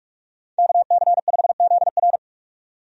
Morse code, heard as a keyed tone, about 46 wpm, 700 Hz. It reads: KC5ZR
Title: USA Call Signs - 5 Characters - 45wpm